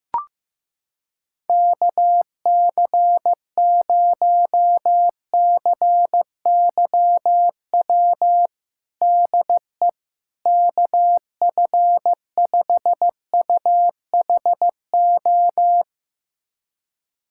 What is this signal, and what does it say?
Morse code, heard as a keyed tone, about 15 wpm, 700 Hz. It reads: KC0CYW DE KF5UHO
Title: USA Callsign DE Callsign - 6 Characters - 15wpm